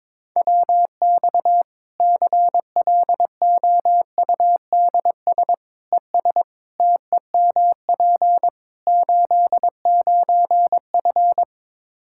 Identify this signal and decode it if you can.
Morse code, heard as a keyed tone, about 22 wpm, 700 Hz. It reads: WX CLOUDS ES TEMP 89F
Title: Practice QSO - Exchange 2 - Info - 22wpm